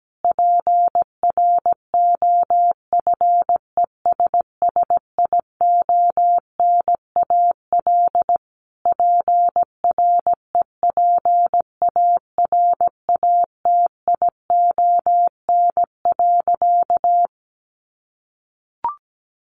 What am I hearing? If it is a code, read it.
Morse code, heard as a keyed tone, about 17 wpm, 700 Hz. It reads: PROFESSIONAL PREPARATION.
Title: Sentences from Top 1400 Words 17wpm